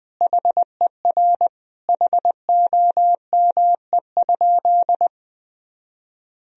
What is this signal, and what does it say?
Morse code, heard as a keyed tone, about 21 wpm, 700 Hz. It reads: HER HOME?
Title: Sentences from Top 1000 Words - Review Part 4 - 20wpm